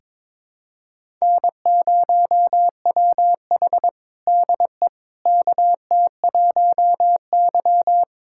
Morse code, heard as a keyed tone, about 22 wpm, 700 Hz. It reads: N0WH DE KT1Y
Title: USA Callsign DE Callsign - 4 Characters - 22wpm